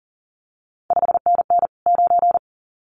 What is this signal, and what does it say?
Morse code, heard as a keyed tone, about 41 wpm, 700 Hz. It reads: 5NN 9
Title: CQ WW DX Contest Exchange Encore - 40wpm